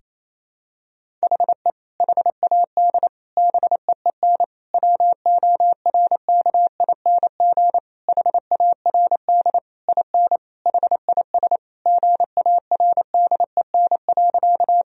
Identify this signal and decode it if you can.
Morse code, heard as a keyed tone, about 28 wpm, 700 Hz. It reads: HE HAD BEEN WORKING HARD IN HIS GARDEN.
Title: Sentences from Top 1000 Words - Review Part 4 - 28wpm